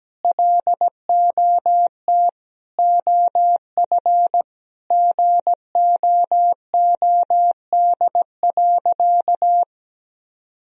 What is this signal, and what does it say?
Morse code, heard as a keyed tone, about 17 wpm, 700 Hz. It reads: LOT OF GOOD.
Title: Sentences from Top 1000 Words - Review Part 3 - 17wpm